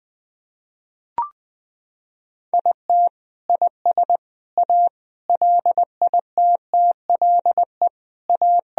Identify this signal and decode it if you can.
Morse code, heard as a keyed tone, about 20 wpm, 700 Hz. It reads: IT IS A LITTLE A
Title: Sentences from Top 1000 Words - Review Part 2 - 20wpm